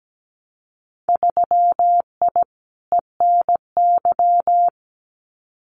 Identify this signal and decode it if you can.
Morse code, heard as a keyed tone, about 17 wpm, 700 Hz. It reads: 3I ENY
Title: Winter Field Day Exchanges Encore 17wpm